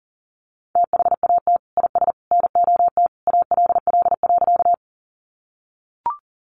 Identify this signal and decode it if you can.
Morse code, heard as a keyed tone, about 40 wpm, 700 Hz. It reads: THAT IS NOT ALL.
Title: Sentences from Top 600 Words 40wpm